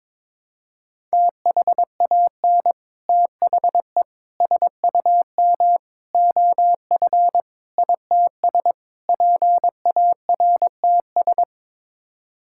Morse code, heard as a keyed tone, about 21 wpm, 700 Hz. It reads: THAN THE SUM OF ITS PARTS
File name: practice / 22wpm / Sets of 6 Words - 22wpm